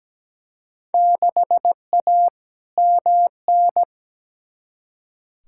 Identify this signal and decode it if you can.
Morse code, heard as a keyed tone, about 17 wpm, 700 Hz. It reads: 6A MN
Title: Field Day Exchanges Encore - 17wpm